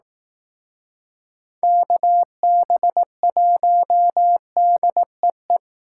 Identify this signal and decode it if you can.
Morse code, heard as a keyed tone, about 19 wpm, 700 Hz. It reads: KB1DEE